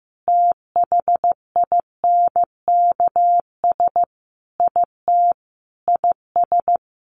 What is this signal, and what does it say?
Morse code, heard as a keyed tone, about 15 wpm, 700 Hz. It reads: THINKS IT IS
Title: Easy Phrases 15wpm